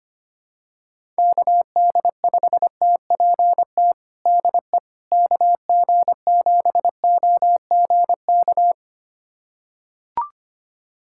Morse code, heard as a keyed tone, about 25 wpm, 700 Hz. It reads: KD5TPT DE KG7OGK
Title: USA Callsign DE Callsign 25wpm